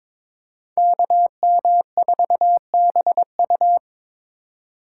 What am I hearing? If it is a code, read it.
Morse code, heard as a keyed tone, about 25 wpm, 700 Hz. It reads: KM4BU
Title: Speed Racing - USA 2x2 Call Signs 22wpm